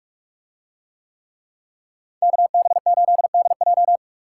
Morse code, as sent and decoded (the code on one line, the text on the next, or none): -.- -... ---.. -.. .---
KB8DJ